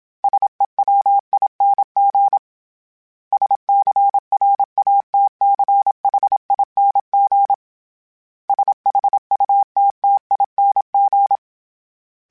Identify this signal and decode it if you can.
Morse code, heard as a keyed tone, about 27 wpm, 800 Hz. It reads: SEWING SCRATCHING SHUTTING
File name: Special INGs-mp3s / Special INGs-mp3s / Sngl Ing @25